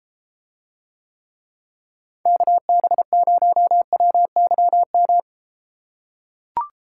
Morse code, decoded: KB0WYM

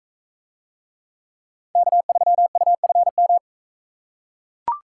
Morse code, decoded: K3UFM

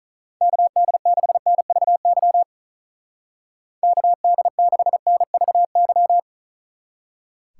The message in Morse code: KD6NVY KD6NVY

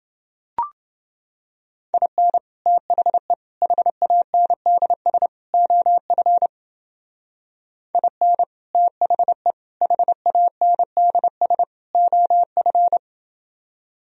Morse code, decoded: IN THE HANDS OF IN THE HANDS OF

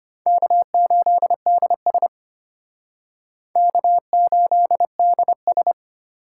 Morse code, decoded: K8DS K8DS

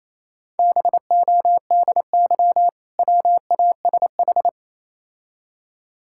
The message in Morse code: BODY WASH

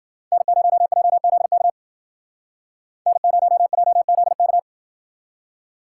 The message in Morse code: N0JZK N0JZK

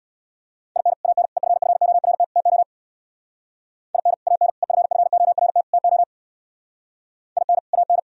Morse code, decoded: IN AN EFFORT TO IN AN EFFORT TO IN AN

